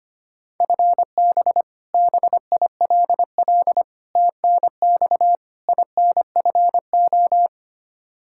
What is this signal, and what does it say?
Morse code, heard as a keyed tone, about 25 wpm, 700 Hz. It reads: FB BILL TNX INFO
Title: Practice QSO - Exchange 3 - More Info - 25wpm